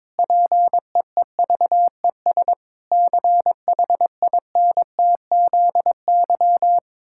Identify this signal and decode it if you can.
Morse code, heard as a keyed tone, about 22 wpm, 700 Hz. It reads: PEEVES CHINTZY
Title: Pangram Challenge v2 - 22wpm